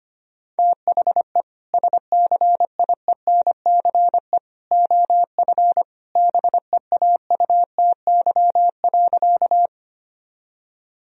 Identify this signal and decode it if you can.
Morse code, heard as a keyed tone, about 25 wpm, 700 Hz. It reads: THE SCIENCE OF BEAUTY.
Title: Sentences from Top 1000 Words - Review Part 2 - 25wpm